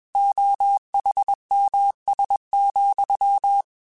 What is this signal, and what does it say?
Morse code, heard as a keyed tone, about 21 wpm, 800 Hz. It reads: OHMS,
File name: Technical Subjects / Technical Subjects-mp3s / XC @20